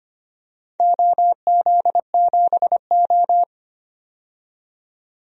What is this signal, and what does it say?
Morse code, heard as a keyed tone, about 25 wpm, 700 Hz. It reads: OZ7O
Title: DXCC-100 Call Signs - 4 Characters - 25wpm